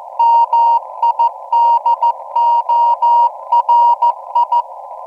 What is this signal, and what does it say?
Morse code, heard as a keyed tone, about 14 wpm, 910 Hz. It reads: MIDORI
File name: UKUBE1-VERT-01